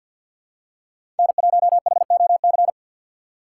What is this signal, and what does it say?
Morse code, heard as a keyed tone, about 49 wpm, 700 Hz. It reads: N1HOC
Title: USA Call Signs - 5 Characters - 6x Character Spacing - 50wpm